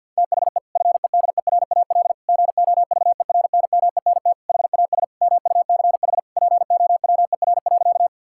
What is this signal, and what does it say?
Morse code, heard as a keyed tone, about 52 wpm, 700 Hz. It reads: THE FEDERAL GOVERNMENT HAS MUCH POWER.
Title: Sentences from Top 1000 Words - Review Part 5 - 50wpm